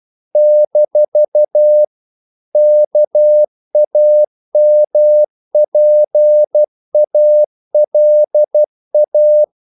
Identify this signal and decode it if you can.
Morse code, heard as a keyed tone, about 12 wpm, 600 Hz. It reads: - KAMPALA
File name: CW / Practice / NationalCapitals12